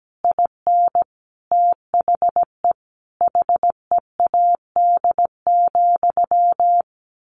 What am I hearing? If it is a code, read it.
Morse code, heard as a keyed tone, about 17 wpm, 700 Hz. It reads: IN THE HEAD,
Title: A Sound Of Thunder 17wpm